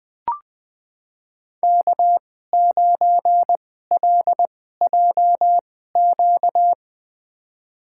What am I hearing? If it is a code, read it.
Morse code, heard as a keyed tone, about 20 wpm, spaced as for 15 wpm, 700 Hz. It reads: K9LJQ
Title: USA Call Signs - 5 Characters - 2x Character Spacing - 20wpm